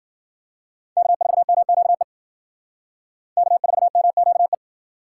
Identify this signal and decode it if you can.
Morse code, heard as a keyed tone, about 53 wpm, 700 Hz. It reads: K4MQE K4MQE